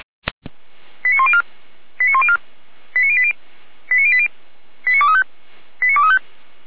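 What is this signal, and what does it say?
Five-tone selective call (ZVEI), heard as quick runs of five tones, tones of about 70 ms. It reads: 89104 89104 89080 89080 89235 89235
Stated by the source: ZVEI five-tone selective call 89120, SDR recording.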